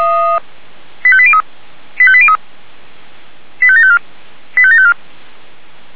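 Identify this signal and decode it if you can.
Five-tone selective call (ZVEI), heard as quick runs of five tones, tones of about 70 ms. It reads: 85803 85803 85754 85754
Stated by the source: Received by SDR.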